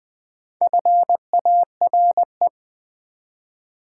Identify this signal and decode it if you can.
Morse code, heard as a keyed tone, about 20 wpm, 700 Hz. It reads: FARE
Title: Build A Word - 1 Letter At A Time - 3 to 5-Letter Words - 20wpm